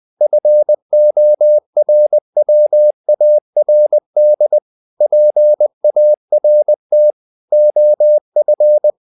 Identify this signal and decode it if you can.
Morse code, heard as a keyed tone, about 20 wpm, 600 Hz. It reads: FORWARD PART OF
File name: the old man and the sea part 2 20wpm